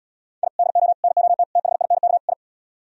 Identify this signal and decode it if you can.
Morse code, heard as a keyed tone, about 77 wpm, 700 Hz. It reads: I DO NOT THINK I